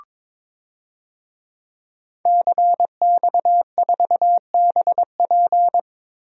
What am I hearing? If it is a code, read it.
Morse code, heard as a keyed tone, about 22 wpm, 700 Hz. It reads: CX4BP
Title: DXCC-100 Call Signs - 5 Characters - Repeated 3x - 22wpm